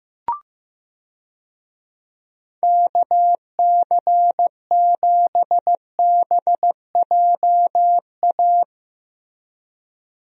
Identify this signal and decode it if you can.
Morse code, heard as a keyed tone, about 15 wpm, 700 Hz. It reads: KC7BJA
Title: USA Call Signs - 6 Characters - Repeated 3x - 15wpm